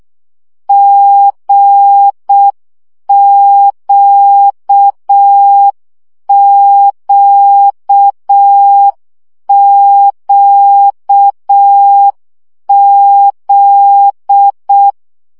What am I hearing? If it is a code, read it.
Morse code, heard as a keyed tone, about 6 wpm, 800 Hz. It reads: GQQQZ
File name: CW gyakorlas g z q 5 baud